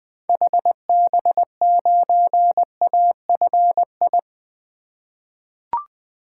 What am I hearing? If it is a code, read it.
Morse code, heard as a keyed tone, about 20 wpm, 700 Hz. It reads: HB9AFI
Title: 2023 SOTA Activator Call Signs - 20wpm